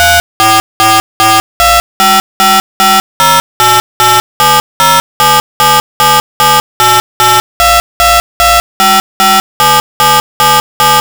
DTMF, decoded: 3444255508870777788222557777